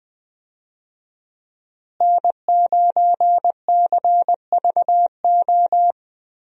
Morse code, heard as a keyed tone, about 20 wpm, 700 Hz. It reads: N9CVO
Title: USA Call Signs - 5 Characters - Repeated 3x - 20wpm